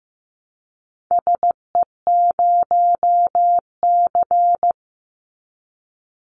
Morse code, heard as a keyed tone, about 15 wpm, 700 Hz. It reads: SE0C
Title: Call Signs Repeated 2x - 15wpm